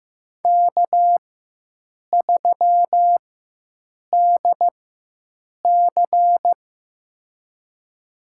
Morse code, heard as a keyed tone, about 15 wpm, spaced as for 7 wpm, 700 Hz. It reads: K3DC
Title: USA Call Signs - 4 Characters - 4x Character Spacing - 15wpm